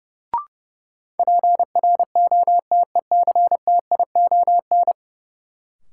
Morse code, heard as a keyed tone, about 30 wpm, 700 Hz. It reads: PROTECTION
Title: Sentences from Top 1600 Words - New Word Drill Alt - 30wpm